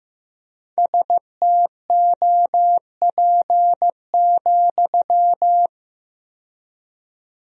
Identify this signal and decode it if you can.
Morse code, heard as a keyed tone, about 15 wpm, 700 Hz. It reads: STOP,
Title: 4x Word-Spacing - Sentences from Top 300 Words - 15wpm